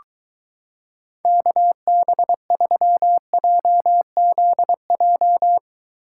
Morse code, heard as a keyed tone, about 23 wpm, 700 Hz. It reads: KB3JZJ